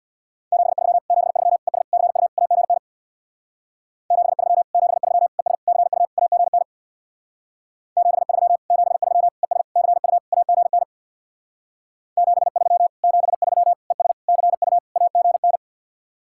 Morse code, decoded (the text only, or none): 73 73 ES CU AGN 73 73 ES CU AGN 73 73 ES CU AGN 73 73 ES CU AGN